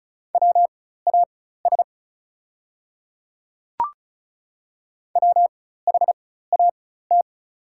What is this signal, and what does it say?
Morse code, heard as a keyed tone, about 36 wpm, spaced as for 16 wpm, 700 Hz. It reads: WAS WHAT
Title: 4x Character Spacing - Top 100 Words - 35wpm